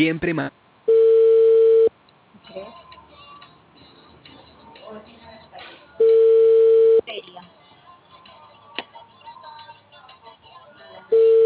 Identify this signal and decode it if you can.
Telephone call progress tone (China, 450 Hz), heard as a telephone network's tone, about 450 Hz, with speech. Ringback tone.